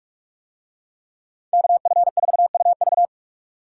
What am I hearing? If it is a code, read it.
Morse code, heard as a keyed tone, about 42 wpm, 700 Hz. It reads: KF4UV